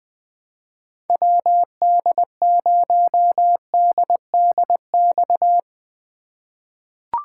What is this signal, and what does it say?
Morse code, heard as a keyed tone, about 20 wpm, 700 Hz. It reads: WD0DDX
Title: USA Call Signs - 6 Characters - 3x Character Spacing - 20wpm